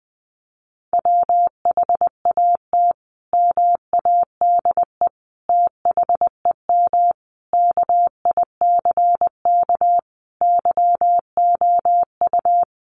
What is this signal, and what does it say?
Morse code, heard as a keyed tone, about 20 wpm, 700 Hz. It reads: WHAT MADE THEM KICK YOU
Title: Sentences from Top 1300 Words 20wpm